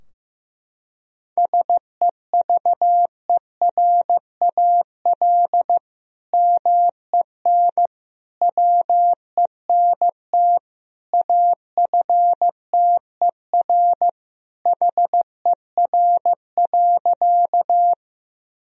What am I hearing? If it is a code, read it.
Morse code, heard as a keyed tone, about 15 wpm, 700 Hz. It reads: SEVERAL MEN WENT AFTER HER.
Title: Sentences from Top 1000 Words - Review Part 2 - 15wpm